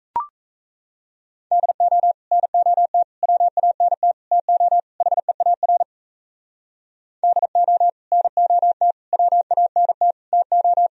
Morse code, heard as a keyed tone, about 36 wpm, 700 Hz. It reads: DO NOT WANT TO HEAR DO NOT WANT TO